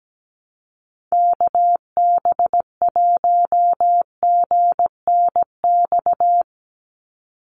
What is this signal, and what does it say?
Morse code, heard as a keyed tone, about 17 wpm, 700 Hz. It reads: KB1GNX
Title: USA Call Signs 17wpm